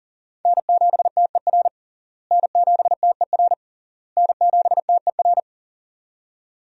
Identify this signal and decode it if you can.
Morse code, heard as a keyed tone, about 37 wpm, 700 Hz. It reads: N7TER N7TER N7TER
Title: USA Call Signs - 5 Characters - Repeated 3x - 40wpm